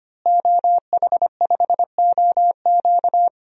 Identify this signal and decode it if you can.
Morse code, heard as a keyed tone, about 25 wpm, 700 Hz. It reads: OH5OQ
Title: DXCC-100 Call Signs - 5 Characters - Repeated 3x - 25wpm